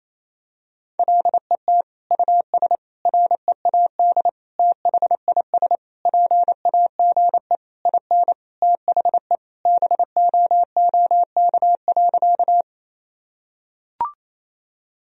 Morse code, heard as a keyed tone, about 28 wpm, 700 Hz. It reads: LET US READ THIS PAGE IN THE BOOK.
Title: Sentences from Top 1000 Words - Review Part 1 - 28wpm